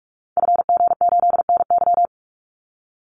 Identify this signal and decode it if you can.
Morse code, heard as a keyed tone, about 44 wpm, 700 Hz. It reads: FG8NY
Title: Call Signs Repeated 2x - 45wpm